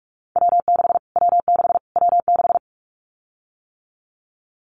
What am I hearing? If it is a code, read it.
Morse code, heard as a keyed tone, about 44 wpm, 700 Hz. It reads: W6 W6 W6
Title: Callsign Prefixes for Top 25 DXCC - Repeated 3x - 45wpm